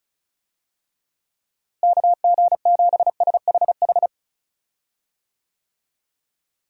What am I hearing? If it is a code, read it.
Morse code, heard as a keyed tone, about 36 wpm, 700 Hz. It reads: KG7SHH
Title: USA Call Signs - 6 Characters - Repeated 3x - 35wpm